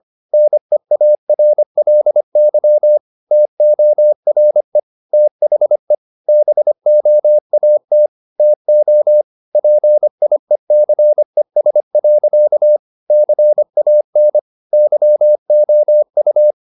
Morse code, heard as a keyed tone, about 25 wpm, 600 Hz. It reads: NEARLY TORE THE BOAT TO PIECES. CAN YOU
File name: the old man and the sea part 1 25wpm